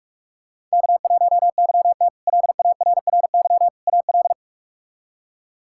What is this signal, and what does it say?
Morse code, heard as a keyed tone, about 42 wpm, 700 Hz. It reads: K1YT LARRY AL
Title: K1USN SST QSO Practice - 45wpm